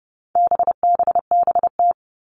Morse code, heard as a keyed tone, about 30 wpm, 700 Hz. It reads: BBBT
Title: All Cut Numbers - 4-Digit 30wpm